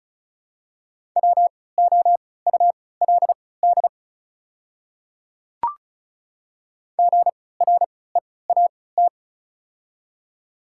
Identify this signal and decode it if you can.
Morse code, heard as a keyed tone, about 36 wpm, spaced as for 20 wpm, 700 Hz. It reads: WOULD GREAT
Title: 3x Character Spacing - Top 300 Words - 35wpm